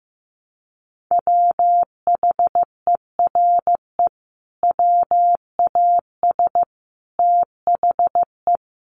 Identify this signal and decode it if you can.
Morse code, heard as a keyed tone, about 15 wpm, 700 Hz. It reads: WHERE WAS THE
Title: Sentences from Top 1000 Words 15wpm